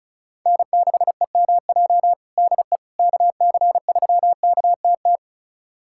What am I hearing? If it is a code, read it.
Morse code, heard as a keyed tone, about 36 wpm, 700 Hz. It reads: N6EMJ DE KC3KTT
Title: USA Callsign DE Callsign 35wpm